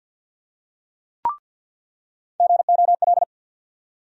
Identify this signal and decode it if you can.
Morse code, heard as a keyed tone, about 50 wpm, 690 Hz. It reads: GOL
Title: USA Call Sign Postfixes - 3 Letters 50wpm